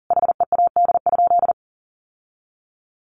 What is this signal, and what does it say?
Morse code, heard as a keyed tone, about 40 wpm, 700 Hz. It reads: HEAD?
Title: Sentences from Top 1500 Words 40wpm